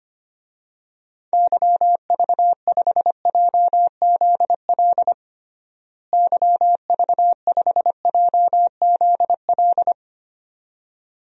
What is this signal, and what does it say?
Morse code, heard as a keyed tone, about 25 wpm, 700 Hz. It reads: YV5JZL YV5JZL
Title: DXCC-100 Call Signs - 6 Characters - Repeated 2x - 25wpm